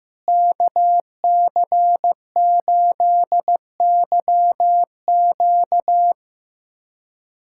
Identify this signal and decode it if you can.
Morse code, heard as a keyed tone, about 15 wpm, 700 Hz. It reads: KC8YQ
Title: USA Callsign DE Callsign - 5 Characters - 15wpm